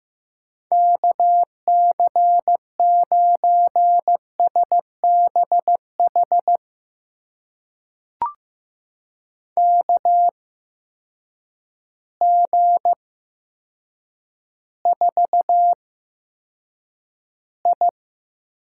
Morse code, heard as a keyed tone, about 15 wpm, spaced as for 8 wpm, 700 Hz. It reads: KC9SBH KG4I
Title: USA Call Signs - 8x Character Spacing - 15wpm